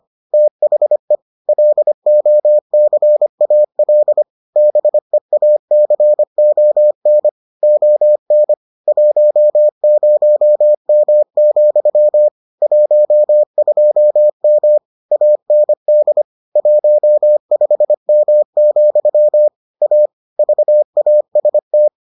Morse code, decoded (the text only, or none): THE LOCAL BEACON ON 10M, 12M AND 15M, A VAST